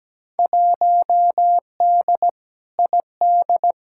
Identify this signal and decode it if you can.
Morse code, heard as a keyed tone, about 17 wpm, 700 Hz. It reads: 1D ID
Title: Field Day Exchanges - 17wpm